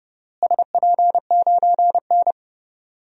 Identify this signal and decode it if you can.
Morse code, heard as a keyed tone, about 30 wpm, 700 Hz. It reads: SP9N
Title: DXCC-100 Call Signs - 4 Characters - Repeated 2x - 30wpm